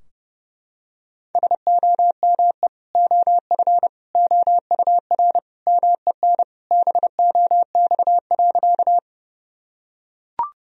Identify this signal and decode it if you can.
Morse code, heard as a keyed tone, about 30 wpm, 700 Hz. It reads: SOME OF OUR MEN BOX.
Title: Sentences from Top 1000 Words - Review Part 2 - 30wpm